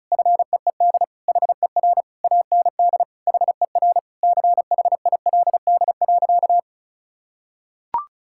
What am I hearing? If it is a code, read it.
Morse code, heard as a keyed tone, about 36 wpm, 700 Hz. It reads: FEED HER AND HER CHILD.
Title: Sentences from Top 1600 Words 35wpm